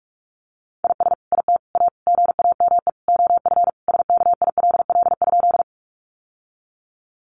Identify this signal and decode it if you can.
Morse code, heard as a keyed tone, about 44 wpm, 700 Hz. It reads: IS IT A GAME OF SKILL?
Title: Sentences from Top 1200 Words 45wpm